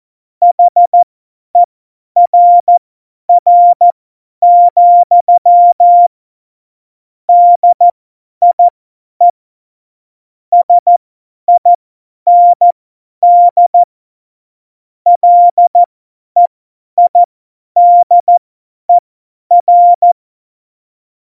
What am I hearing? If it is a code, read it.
Morse code, heard as a keyed tone, about 14 wpm, spaced as for 10 wpm, 700 Hz. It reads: HERR, DIE SIND LEIDER
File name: Morsekurs / Witze / Witz 02. April 2019, 50 F70